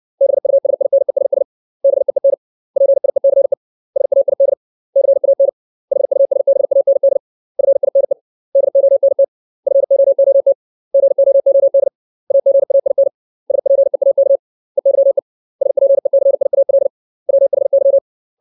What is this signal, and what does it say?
Morse code, heard as a keyed tone, about 60 wpm, 550 Hz. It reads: BUSINESS BEEN PIECE SAID CAN HUSBAND FINE DONT FOOT GOOD AGAIN SPEAK EYE SPECIAL WHY
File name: 632 words / 632 Words 60WPM